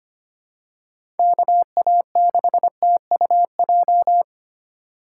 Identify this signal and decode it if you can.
Morse code, heard as a keyed tone, about 25 wpm, 700 Hz. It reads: KA6TUJ
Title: USA Call Signs - 8x Character Spacing - 25wpm